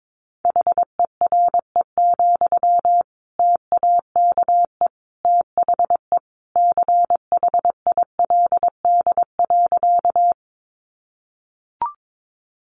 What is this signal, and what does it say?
Morse code, heard as a keyed tone, about 22 wpm, 700 Hz. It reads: HERE, TAKE THE CHILD.
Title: Sentences from Top 800 Words 22wpm